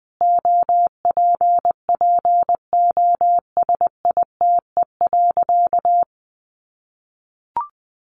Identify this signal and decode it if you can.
Morse code, heard as a keyed tone, about 20 wpm, 700 Hz. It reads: OPPOSITE.
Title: Sentences from Top 1300 Words 20wpm